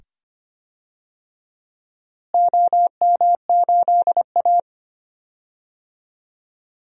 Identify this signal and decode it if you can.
Morse code, heard as a keyed tone, about 25 wpm, 700 Hz. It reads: OM8A
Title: DXCC-100 Call Signs - 4 Characters - Repeated 2x - 25wpm